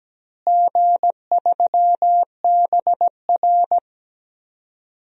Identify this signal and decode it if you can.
Morse code, heard as a keyed tone, about 17 wpm, 700 Hz. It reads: G3BR
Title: DXCC-100 Call Signs - 4 Characters - Repeated 3x - 17wpm